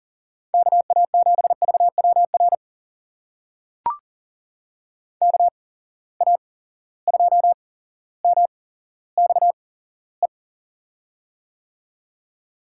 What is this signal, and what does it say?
Morse code, heard as a keyed tone, about 40 wpm, spaced as for 17 wpm, 700 Hz. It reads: KA7VWR KA2MXE
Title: USA Call Signs - 6 Characters - 8x Character Spacing - 40wpm